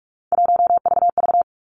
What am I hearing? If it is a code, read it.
Morse code, heard as a keyed tone, about 46 wpm, 700 Hz. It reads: 1VV
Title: Three Challenging Letter-Number Combinations 45wpm